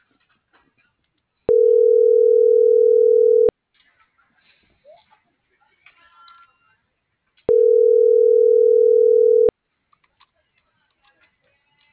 Ringback tone (North America), 440 + 480 Hz.